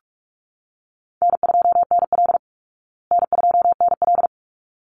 Morse code, heard as a keyed tone, about 44 wpm, 700 Hz. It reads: N2NL N2NL
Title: Easy Call Signs Repeated 2x - 45wpm